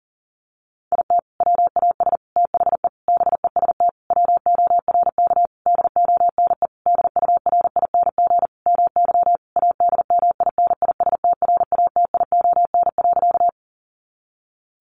Morse code, heard as a keyed tone, about 40 wpm, 700 Hz. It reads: IT WAS THE BEST WORK DONE DURING MY ADMINISTRATION.
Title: Sentences from Top 1300 Words 40wpm